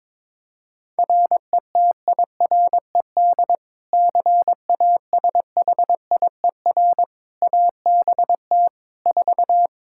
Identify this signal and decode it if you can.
Morse code, heard as a keyed tone, about 22 wpm, 700 Hz. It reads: RETIRED CASHIER ABT 4
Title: Practice QSO - Exchange 3 - More Info - 22wpm